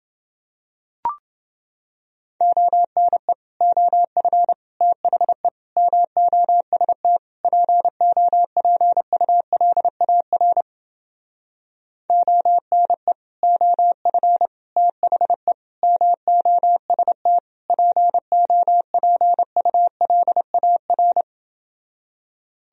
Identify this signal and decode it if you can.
Morse code, heard as a keyed tone, about 29 wpm, 700 Hz. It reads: ONE OF THE MOST POPULAR ONE OF THE MOST POPULAR